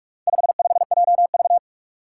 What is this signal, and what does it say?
Morse code, heard as a keyed tone, about 42 wpm, 700 Hz. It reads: 55JV
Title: Four Challenging Letter-Number Combinations 45wpm